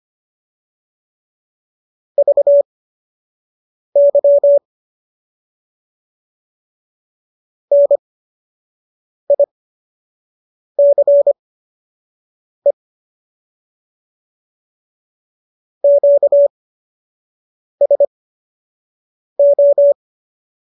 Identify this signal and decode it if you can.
Morse code, heard as a keyed tone, about 25 wpm, spaced as for 6 wpm, 570 Hz. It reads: VY NICE QSO